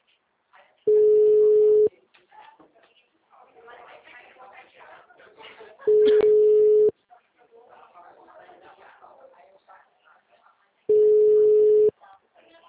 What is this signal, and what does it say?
Ringback tone (Europe, 425 Hz), with speech.